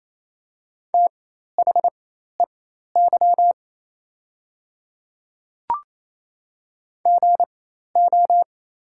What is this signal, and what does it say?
Morse code, heard as a keyed tone, about 28 wpm, spaced as for 13 wpm, 700 Hz. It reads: THEY GO
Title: 4x Character Spacing - Top 300 Words - 28wpm